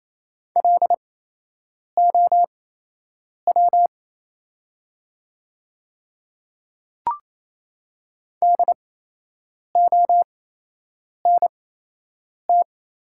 Morse code, heard as a keyed tone, about 28 wpm, spaced as for 8 wpm, 700 Hz. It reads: LOW DONT
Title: 8x Character Spacing - Top 300 Words - 28wpm